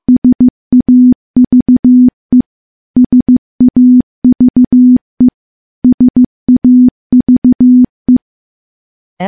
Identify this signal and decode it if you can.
Morse code, heard as a keyed tone, about 15 wpm, 250 Hz. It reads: SAVE SAVE SAVE
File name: morse / morse-training-15WPM